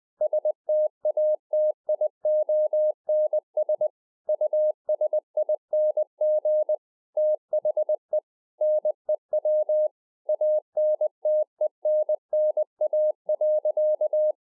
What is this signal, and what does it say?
Morse code, heard as a keyed tone, about 20 wpm, 610 Hz. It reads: STATIONS USING THE NEW ANTENNA.